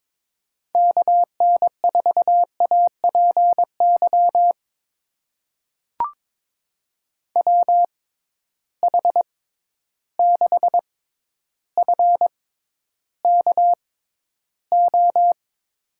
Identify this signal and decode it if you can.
Morse code, heard as a keyed tone, about 22 wpm, spaced as for 11 wpm, 700 Hz. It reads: KN4APY WH6FKO
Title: USA Call Signs - 6 Characters - 6x Character Spacing - 22wpm